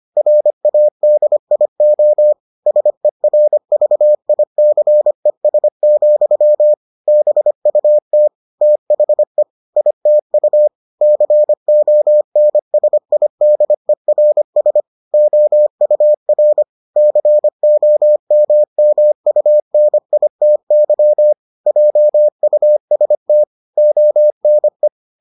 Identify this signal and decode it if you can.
Morse code, heard as a keyed tone, about 25 wpm, 600 Hz. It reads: RADIO SERVICES, BUT THE ITU CONSIDERS OUR COMMUNITY JUST ONE